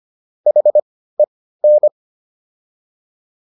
Morse code, heard as a keyed tone, about 25 wpm, spaced as for 15 wpm, 600 Hz. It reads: HEN